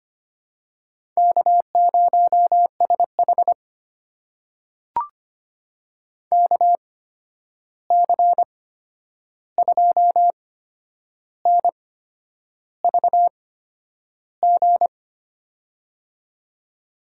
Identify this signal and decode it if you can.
Morse code, heard as a keyed tone, about 25 wpm, spaced as for 9 wpm, 700 Hz. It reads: K0SH KC2NVG
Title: USA Call Signs - 8x Character Spacing - 25wpm